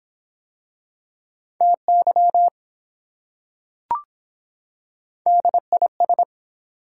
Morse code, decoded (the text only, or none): TY DIS